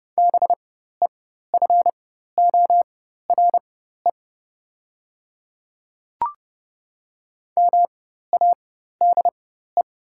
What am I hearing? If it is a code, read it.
Morse code, heard as a keyed tone, about 30 wpm, spaced as for 14 wpm, 700 Hz. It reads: BEFORE MADE
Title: 4x Character Spacing - Top 300 Words - 30wpm